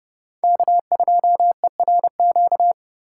Morse code, heard as a keyed tone, about 36 wpm, 700 Hz. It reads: K2ERQ
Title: Speed Racing - USA Call Signs 30wpm